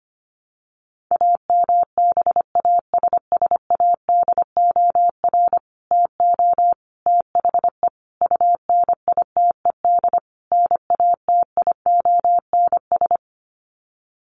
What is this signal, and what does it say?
Morse code, heard as a keyed tone, about 25 wpm, 700 Hz. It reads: AMBASSADOR TO THE UNITED NATIONS
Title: Sets of 5 Words Encore2 25wpm